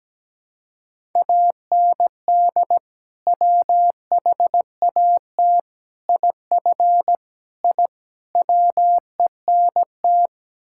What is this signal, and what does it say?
Morse code, heard as a keyed tone, about 17 wpm, 700 Hz. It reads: AND WHAT IF I WENT
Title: Sentences from Top 1000 Words - Review Part 5 - 17wpm